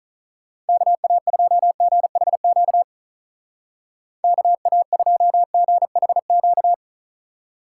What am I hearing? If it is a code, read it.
Morse code, heard as a keyed tone, about 44 wpm, 700 Hz. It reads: KA2GHQ KA2GHQ
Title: Speed Racing - USA 2x3 Call Signs 35wpm